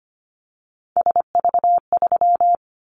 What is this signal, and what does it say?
Morse code, heard as a keyed tone, about 25 wpm, 690 Hz. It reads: SV3
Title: Callsign Prefixes for Top 50 DXCC - 25wpm